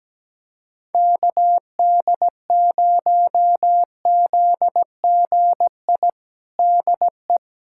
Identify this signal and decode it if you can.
Morse code, heard as a keyed tone, about 17 wpm, 700 Hz. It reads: KD0ZGI DE
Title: USA Callsign DE Callsign - 6 Characters - 17wpm